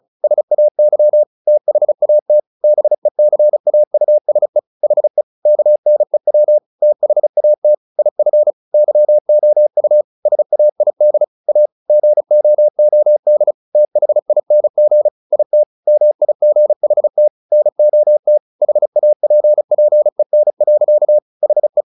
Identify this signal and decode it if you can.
Morse code, heard as a keyed tone, about 34 wpm, 600 Hz. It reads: SAY THAT BECAUSE HE KNEW THAT IF YOU SAID A GOOD THING IT MIGHT NOT HAPPEN. HE
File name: the old man and the sea / the old man and the sea 1 35wpm